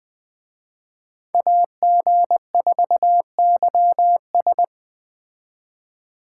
Morse code, heard as a keyed tone, about 20 wpm, 700 Hz. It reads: AG4YS